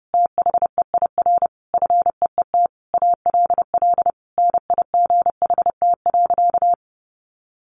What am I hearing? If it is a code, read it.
Morse code, heard as a keyed tone, about 30 wpm, 700 Hz. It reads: THEIR FEET ALL NIGHT.
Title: Sentences from Top 1500 Words 30wpm